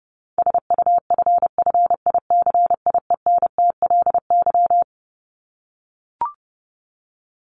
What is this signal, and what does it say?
Morse code, heard as a keyed tone, about 30 wpm, 700 Hz. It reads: SUFFICIENTLY
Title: LY Suffix - Top 100 Words with Base Words 30wpm